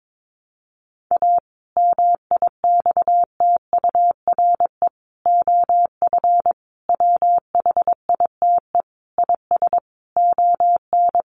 Morse code, heard as a keyed tone, about 22 wpm, 700 Hz. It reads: A MIXTURE OF WHITE IS ON
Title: Sentences from Top 1500 Words 22wpm